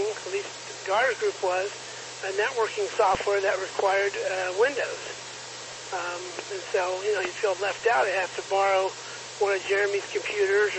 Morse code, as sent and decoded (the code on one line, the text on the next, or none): none